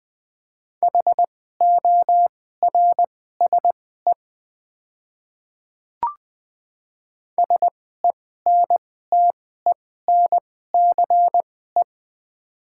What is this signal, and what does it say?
Morse code, heard as a keyed tone, about 20 wpm, spaced as for 14 wpm, 700 Hz. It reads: HORSE SENTENCE
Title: 2x Character Spacing - Top 300 Words - 20wpm